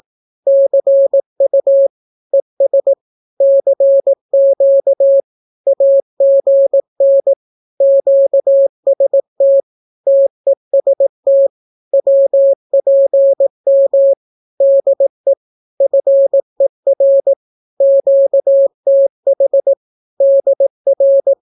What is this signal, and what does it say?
Morse code, heard as a keyed tone, about 18 wpm, 550 Hz. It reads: CU ES CQ AGN QST TEST WPM DE FER QTH DR
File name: naukapodstaw / skroty / skroty